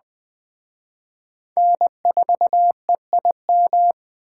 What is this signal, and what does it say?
Morse code, heard as a keyed tone, about 21 wpm, 700 Hz. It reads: N4EIM
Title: Speed Racing - USA Call Signs 17wpm